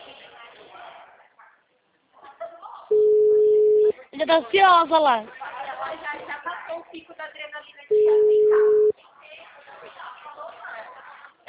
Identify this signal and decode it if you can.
Ringback tone (Europe, 425 Hz), with speech.